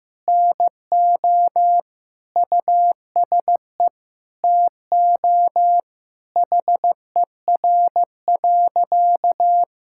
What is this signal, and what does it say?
Morse code, heard as a keyed tone, about 15 wpm, 700 Hz. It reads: NO USE TO HER.
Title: Sentences from Top 1000 Words - Review Part 6 - 15wpm